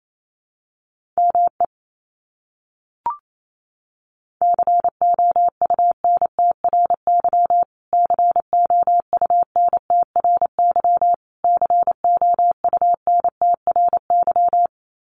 Morse code, heard as a keyed tone, about 28 wpm, 700 Hz. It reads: ME COUNTRY COUNTRY COUNTRY
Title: Top 300 Words Repeated 28wpm